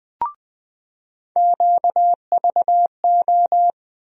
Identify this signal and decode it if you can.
Morse code, heard as a keyed tone, about 20 wpm, 700 Hz. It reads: QVO
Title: USA Call Sign Postfixes - 3 Letters 20wpm